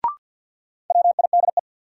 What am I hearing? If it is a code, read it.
Morse code, heard as a keyed tone, about 49 wpm, 700 Hz. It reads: WIDE
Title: Sentences from Top 900 Words - New Word Drill - 50wpm